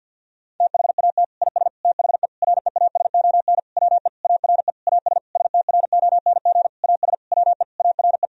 Morse code, heard as a keyed tone, about 50 wpm, 700 Hz. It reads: THAT IS THE REASON WE ARE AS STRONG AS WE ARE
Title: Sentences from Top 1000 Words - Review Part 4 - 50wpm